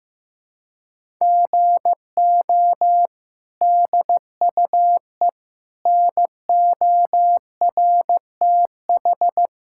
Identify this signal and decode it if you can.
Morse code, heard as a keyed tone, about 15 wpm, 700 Hz. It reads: GO DUE NORTH
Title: Sentences from Top 1000 Words - Review Part 6 - 15wpm